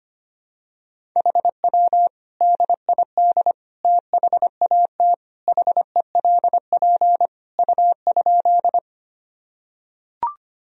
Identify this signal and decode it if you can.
Morse code, heard as a keyed tone, about 25 wpm, 700 Hz. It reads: HW DID THAT HELP U?
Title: Sentences from Top 200 Words and Abbreviations 25wpm